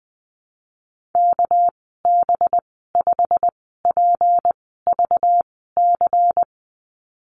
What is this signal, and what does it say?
Morse code, heard as a keyed tone, about 20 wpm, spaced as for 15 wpm, 700 Hz. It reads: KB5PVC
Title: 200 Lesson 32g - Letter k - Call Signs 20wpm